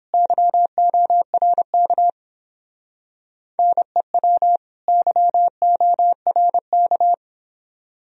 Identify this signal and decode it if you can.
Morse code, heard as a keyed tone, about 28 wpm, 700 Hz. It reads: YORK NEW YORK